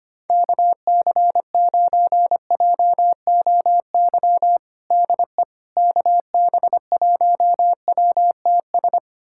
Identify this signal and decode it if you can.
Morse code, heard as a keyed tone, about 25 wpm, 700 Hz. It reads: KC9JOY DE KB1WTS
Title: USA Callsign DE Callsign - 6 Characters - 25wpm